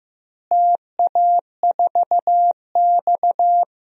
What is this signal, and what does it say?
Morse code, heard as a keyed tone, about 15 wpm, 700 Hz. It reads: TA4X
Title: DXCC-100 Call Signs - 4 Characters - Repeated 3x - 15wpm